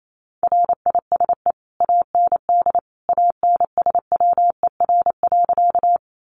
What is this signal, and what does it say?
Morse code, heard as a keyed tone, about 28 wpm, 700 Hz. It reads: RISE AND ANSWER.
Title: Sentences from Top 900 Words 28wpm